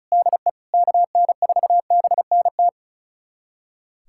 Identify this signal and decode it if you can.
Morse code, heard as a keyed tone, about 34 wpm, 700 Hz. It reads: DE KN4BNT
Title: USA Callsign DE Callsign - 6 Characters - 35wpm